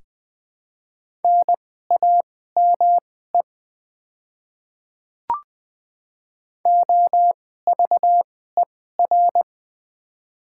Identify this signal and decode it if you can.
Morse code, heard as a keyed tone, about 20 wpm, spaced as for 14 wpm, 700 Hz. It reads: NAME OVER
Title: 2x Character Spacing - Top 500 Words - 20wpm